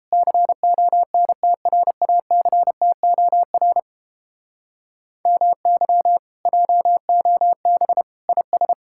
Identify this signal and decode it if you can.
Morse code, heard as a keyed tone, about 29 wpm, 700 Hz. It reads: CONTRACTOR MY JOB IS